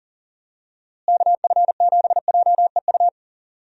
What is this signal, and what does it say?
Morse code, heard as a keyed tone, about 40 wpm, 700 Hz. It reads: KF7JEU
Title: USA Call Signs - 6 Characters - Repeated 3x - 40wpm